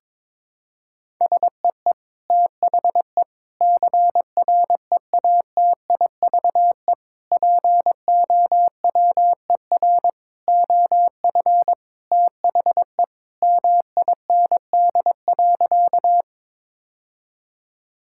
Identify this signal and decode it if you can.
Morse code, heard as a keyed tone, about 22 wpm, 700 Hz. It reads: SEE THE CREATIVE POWER OF THE MIND.
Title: Sentences from Top 1000 Words - Review Part 5 - 22wpm